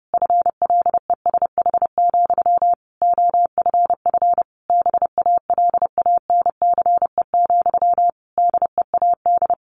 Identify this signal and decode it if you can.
Morse code, heard as a keyed tone, about 30 wpm, 700 Hz. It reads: FLESH, OFF BALANCE, DEAD
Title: A Sound Of Thunder 30wpm